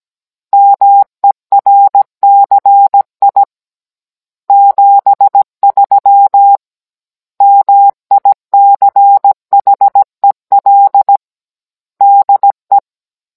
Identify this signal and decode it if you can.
Morse code, heard as a keyed tone, about 17 wpm, 800 Hz. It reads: MERCI 73 MICHEL DE
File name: Cours-NR18-16mm